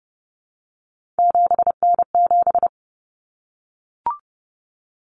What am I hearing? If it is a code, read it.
Morse code, heard as a keyed tone, about 30 wpm, 700 Hz. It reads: 7N7
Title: Common Cut Numbers - 3-Digit 30wpm